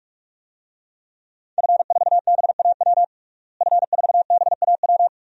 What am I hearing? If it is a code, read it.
Morse code, heard as a keyed tone, about 46 wpm, 700 Hz. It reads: F4BAW F4BAW
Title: DXCC-100 Call Signs - 5 Characters - Repeated 3x - 45wpm